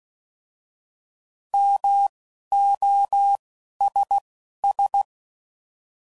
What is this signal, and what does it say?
Morse code, heard as a keyed tone, about 16 wpm, spaced as for 11 wpm, 800 Hz. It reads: MOSS